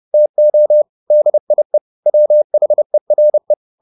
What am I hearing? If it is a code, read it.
Morse code, heard as a keyed tone, about 30 wpm, 600 Hz. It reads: TO DIE WHERE